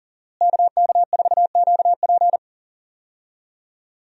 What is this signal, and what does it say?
Morse code, heard as a keyed tone, about 40 wpm, 690 Hz. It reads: KK4QP
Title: USA Call Signs - 5 Characters - Repeated 3x - 40wpm